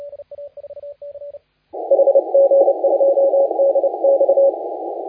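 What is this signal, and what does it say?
Morse code, heard as a keyed tone, about 30 wpm, 550 Hz, with noise. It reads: HG2DX
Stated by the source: Morse contest audio recorded by LA4C during the SAC CW 2018 contest.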